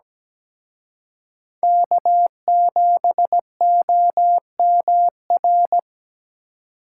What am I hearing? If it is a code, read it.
Morse code, heard as a keyed tone, about 21 wpm, 700 Hz. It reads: K7OMR
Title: Speed Racing - USA 1x3 Call Signs 17wpm